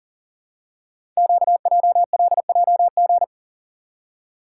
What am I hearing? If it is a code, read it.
Morse code, heard as a keyed tone, about 42 wpm, 700 Hz. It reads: QJLJG
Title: Code Groups - 5 Letters - 40wpm